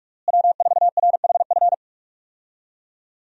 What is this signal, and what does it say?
Morse code, heard as a keyed tone, about 46 wpm, 700 Hz. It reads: W4RHF
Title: USA Call Signs - 5 Characters - Repeated 3x - 45wpm